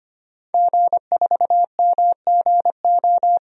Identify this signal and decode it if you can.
Morse code, heard as a keyed tone, about 25 wpm, 700 Hz. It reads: G4MGO
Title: DXCC-100 Call Signs - 5 Characters - Repeated 2x - 25wpm